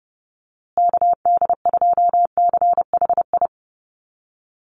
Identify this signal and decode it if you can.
Morse code, heard as a keyed tone, about 30 wpm, 700 Hz. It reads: KD2CHI
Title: USA Call Signs Repeated 2x 30wpm